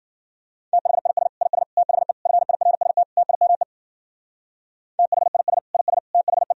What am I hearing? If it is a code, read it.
Morse code, heard as a keyed tone, about 50 wpm, 700 Hz. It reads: THIS IS THE FIRST TIME THIS IS THE
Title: Speed-Racing - Sets of 5 Words 40wpm